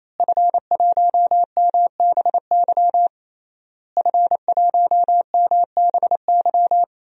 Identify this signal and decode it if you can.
Morse code, heard as a keyed tone, about 27 wpm, 700 Hz. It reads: F1MBY F1MBY
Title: DXCC-100 Call Signs - 5 Characters - Repeated 3x - 28wpm